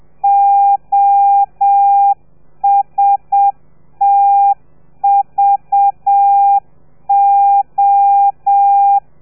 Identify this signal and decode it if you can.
Morse code, heard as a keyed tone, about 7 wpm, 800 Hz. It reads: OSTVO
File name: telegrafia / i2tfj / Lez4 30